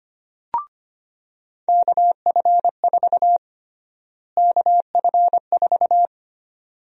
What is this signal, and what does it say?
Morse code, heard as a keyed tone, about 25 wpm, 700 Hz. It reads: KF4 KF4
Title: USA Call Sign Prefixes - Repeated 2x - 25wpm